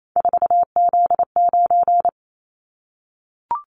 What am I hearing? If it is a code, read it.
Morse code, heard as a keyed tone, about 28 wpm, 700 Hz. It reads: 4Z9
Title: Callsign Prefixes for Top 50 DXCC - 28wpm